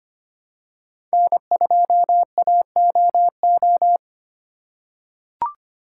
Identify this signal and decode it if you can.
Morse code, heard as a keyed tone, about 25 wpm, 700 Hz. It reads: N2AOO
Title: USA Call Signs - 5 Characters - 4x Character Spacing - 25wpm